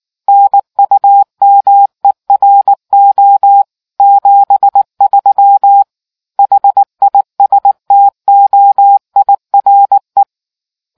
Morse code, heard as a keyed tone, about 19 wpm, 800 Hz. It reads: NUMERO 73 HISTOIRE